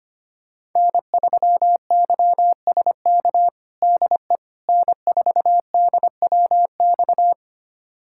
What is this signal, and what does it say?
Morse code, heard as a keyed tone, about 25 wpm, 700 Hz. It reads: N3YSK DE N4DWX
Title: USA Callsign DE Callsign - 5 Characters - 25wpm